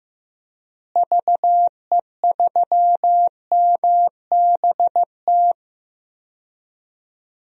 Morse code, heard as a keyed tone, about 15 wpm, 700 Hz. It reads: VE3MBT
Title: DXCC-100 Call Signs - 6 Characters - Repeated 3x - 15wpm